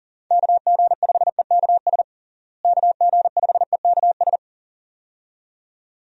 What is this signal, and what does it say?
Morse code, heard as a keyed tone, about 40 wpm, 700 Hz. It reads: KG5EKS KG5EKS
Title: USA Call Signs - 6 Characters - Repeated 2x - 40wpm